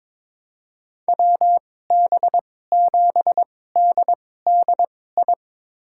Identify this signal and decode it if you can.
Morse code, heard as a keyed tone, about 21 wpm, spaced as for 16 wpm, 700 Hz. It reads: WB7DDI
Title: USA Call Signs - 6 Characters - 2x Character Spacing - 22wpm